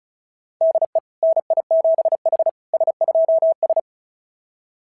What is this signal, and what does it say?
Morse code, heard as a keyed tone, about 36 wpm, 660 Hz. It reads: DE NI7H S2S
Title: SOTA QSO Practice 35wpm